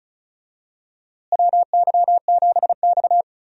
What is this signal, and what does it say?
Morse code, heard as a keyed tone, about 34 wpm, 700 Hz. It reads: WY7X
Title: USA Call Signs - 4 Characters - Repeated 3x - 35wpm